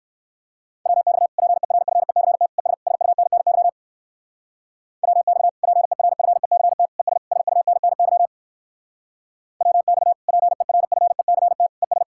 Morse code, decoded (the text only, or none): WX PERFECT ES SUNNY WX PERFECT ES SUNNY WX PERFECT ES